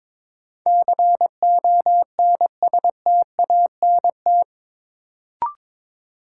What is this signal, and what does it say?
Morse code, heard as a keyed tone, about 21 wpm, 700 Hz. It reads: CONSTANT
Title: Sentences from Top 1500 Words - New Word Drill - 22wpm